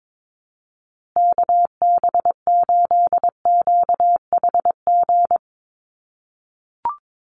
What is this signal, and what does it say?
Morse code, heard as a keyed tone, about 22 wpm, 700 Hz. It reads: KB8QHG
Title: USA Call Signs Repeated 2x 22wpm